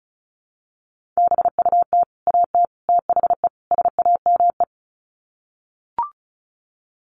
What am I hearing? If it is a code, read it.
Morse code, heard as a keyed tone, about 34 wpm, 700 Hz. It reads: BUT AT THE SAME
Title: Sets of 4 Words 35wpm